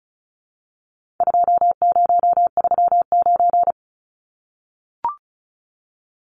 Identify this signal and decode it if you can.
Morse code, heard as a keyed tone, about 34 wpm, 700 Hz. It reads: 2039
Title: Numbers - 4-Digit 35wpm